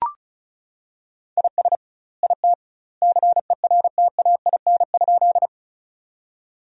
Morse code, heard as a keyed tone, about 34 wpm, 700 Hz. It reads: IS IT CERTAIN?
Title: 2x Word-Spacing - Sentences from Top 500 Words - 35wpm